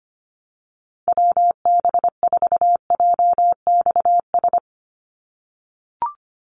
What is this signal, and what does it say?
Morse code, heard as a keyed tone, about 25 wpm, 700 Hz. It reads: WB4JXS
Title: USA Call Signs 25wpm